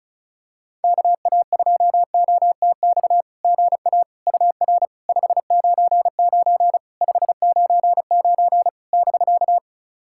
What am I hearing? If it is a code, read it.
Morse code, heard as a keyed tone, about 34 wpm, 700 Hz. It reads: KA2OTX GA UR 599 599 <BK>
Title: SOTA QSO Practice 35wpm